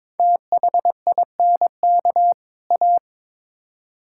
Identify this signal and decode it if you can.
Morse code, heard as a keyed tone, about 22 wpm, 700 Hz. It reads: THINK A
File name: practice / 22wpm / Build A Phrase - 5-Word Phrases - 22wpm